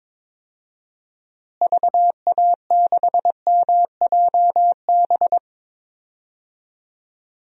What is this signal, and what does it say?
Morse code, heard as a keyed tone, about 22 wpm, 700 Hz. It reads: VA6MJB
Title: DXCC-100 Call Signs - 6 Characters - Repeated 2x - 22wpm